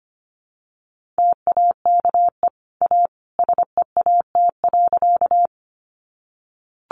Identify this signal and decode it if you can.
Morse code, heard as a keyed tone, about 25 wpm, 700 Hz. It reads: TAKE A SEAT.
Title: Sentences from Top 1500 Words 25wpm